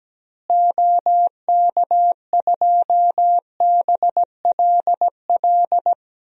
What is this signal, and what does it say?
Morse code, heard as a keyed tone, about 17 wpm, 700 Hz. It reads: OK2BLL
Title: DXCC-100 Call Signs - 6 Characters - Repeated 3x - 17wpm